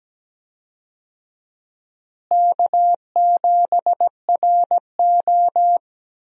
Morse code, heard as a keyed tone, about 17 wpm, 700 Hz. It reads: K7RO
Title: USA Call Signs - 4 Characters - Repeated 2x - 17wpm